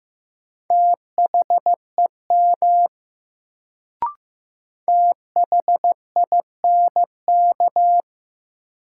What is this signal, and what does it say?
Morse code, heard as a keyed tone, about 15 wpm, 700 Hz. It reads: THEM THINK
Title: Sets of 2 Words - New Word Drill - 15wpm